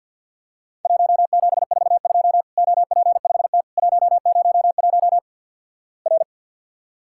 Morse code, heard as a keyed tone, about 51 wpm, 700 Hz. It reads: 1742 OP5T 101 R
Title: Worked All Europe DX Contest - QTC Exchange with Repeats - 50wpm